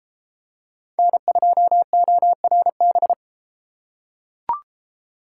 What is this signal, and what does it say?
Morse code, heard as a keyed tone, about 31 wpm, 700 Hz. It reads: N2ORB